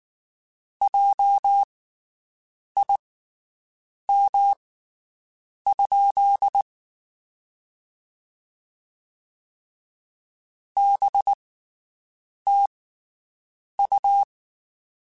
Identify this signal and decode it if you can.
Morse code, heard as a keyed tone, about 19 wpm, spaced as for 7 wpm, 800 Hz. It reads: JIM? BTU
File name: (spaced) / QSOs Spaced-mp3s / WD0BGP 1 @18 sp